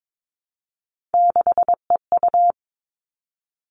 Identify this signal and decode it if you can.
Morse code, heard as a keyed tone, about 22 wpm, 700 Hz. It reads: 6EU
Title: All Cut Numbers - 3-Digit 22wpm